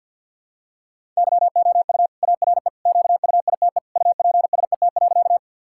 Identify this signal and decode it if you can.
Morse code, heard as a keyed tone, about 50 wpm, 700 Hz. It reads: YOU ARE QUITE UPSET.
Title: Sentences from Top 1000 Words - Review Part 5 - 50wpm